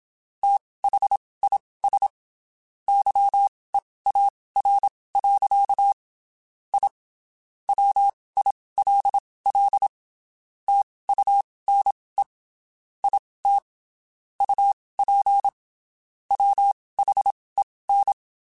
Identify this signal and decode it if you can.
Morse code, heard as a keyed tone, about 27 wpm, spaced as for 19 wpm, 800 Hz. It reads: THIS YEAR. I WILL TUNE IT UP WHEN
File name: (spaced) / QSOs Spaced-mp3s / WD0FBT @25 sp